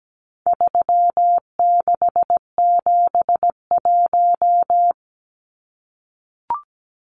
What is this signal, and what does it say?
Morse code, heard as a keyed tone, about 17 wpm, 700 Hz. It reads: 3671
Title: Numbers - 4-Digit 17wpm